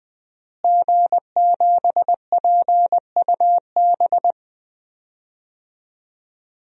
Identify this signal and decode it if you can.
Morse code, heard as a keyed tone, about 20 wpm, 700 Hz. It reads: G7PUB
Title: DXCC-100 Call Signs - 5 Characters - Repeated 3x - 20wpm